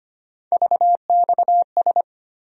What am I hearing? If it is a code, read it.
Morse code, heard as a keyed tone, about 25 wpm, 700 Hz. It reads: VXS
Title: USA Call Sign Postfixes - 3 Letters 25wpm